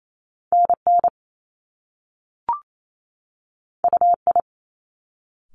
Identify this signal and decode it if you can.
Morse code, heard as a keyed tone, about 28 wpm, 700 Hz. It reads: NN UI
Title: Top Two Letter Combinations in Words 28wpm